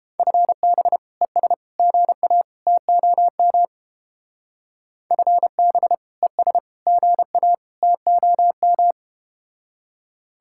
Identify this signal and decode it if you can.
Morse code, heard as a keyed tone, about 29 wpm, 700 Hz. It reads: FB ES GA TOM FB ES GA TOM